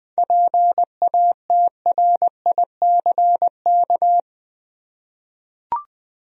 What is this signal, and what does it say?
Morse code, heard as a keyed tone, about 20 wpm, 700 Hz. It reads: PATRICK
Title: USA Common Names - 20wpm